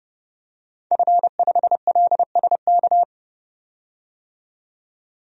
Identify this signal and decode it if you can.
Morse code, heard as a keyed tone, about 32 wpm, 700 Hz. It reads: F5LSK
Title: DXCC-100 Call Signs - 5 Characters - 30wpm